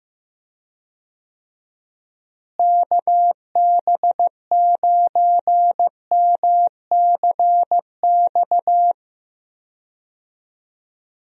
Morse code, heard as a keyed tone, about 15 wpm, 700 Hz. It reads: KB9MCX
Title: USA Call Signs - 6 Characters - Repeated 3x - 15wpm